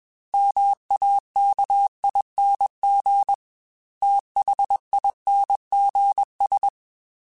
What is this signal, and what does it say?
Morse code, heard as a keyed tone, about 21 wpm, 800 Hz. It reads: MAKING THINGS
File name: Technical Subjects / Technical Subjects-mp3s / AND Gate @20